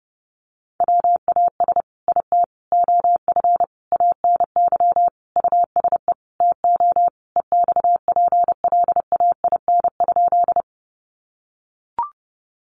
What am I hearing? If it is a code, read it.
Morse code, heard as a keyed tone, about 30 wpm, 700 Hz. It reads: WAS IT OF ANY USE TO EXPLAIN?
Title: Sentences from Top 1100 Words 30wpm